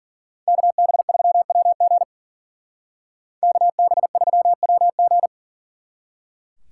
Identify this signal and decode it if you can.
Morse code, heard as a keyed tone, about 47 wpm, 700 Hz. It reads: KB3WG KB3WG
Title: Speed Racing - USA 2x2 Call Signs 40wpm